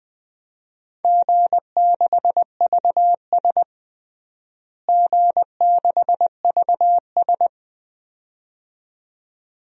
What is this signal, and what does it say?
Morse code, heard as a keyed tone, about 21 wpm, 700 Hz. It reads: G6VS G6VS
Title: DXCC-100 Call Signs - 4 Characters - Repeated 2x - 20wpm